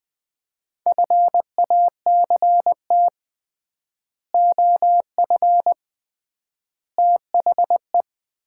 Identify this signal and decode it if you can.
Morse code, heard as a keyed tone, about 20 wpm, 700 Hz. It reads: FACT OF THE
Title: 3x Word-Spacing - Sets of 5 Words - 20wpm